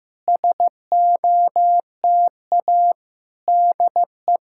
Morse code, heard as a keyed tone, about 15 wpm, 700 Hz. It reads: SOTA DE
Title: SOTA QSO Practice 15wpm